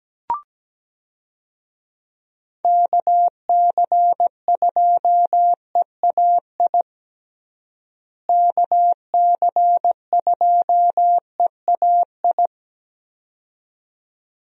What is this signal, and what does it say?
Morse code, heard as a keyed tone, about 17 wpm, 700 Hz. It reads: KC2EAI KC2EAI
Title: USA Call Signs - 6 Characters - Repeated 2x - 17wpm